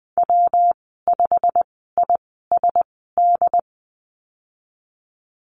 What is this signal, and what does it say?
Morse code, heard as a keyed tone, about 20 wpm, spaced as for 15 wpm, 700 Hz. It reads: W5ISD
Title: 186 Lesson 30g - Letter b - Call Signs 20wpm